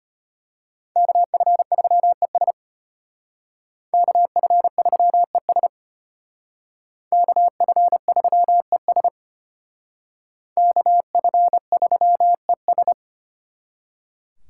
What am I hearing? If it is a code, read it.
Morse code, heard as a keyed tone, about 32 wpm, 700 Hz. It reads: KF3ES KF3ES KF3ES KF3ES